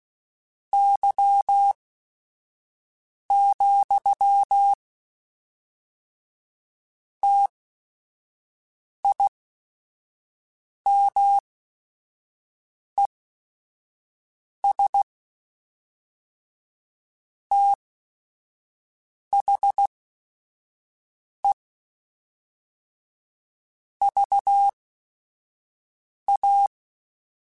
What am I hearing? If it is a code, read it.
Morse code, heard as a keyed tone, about 16 wpm, spaced as for 5 wpm, 800 Hz. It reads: Y, TIMES THE VA
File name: Technical Subjects / Technical Subjects-mp3s / XC @5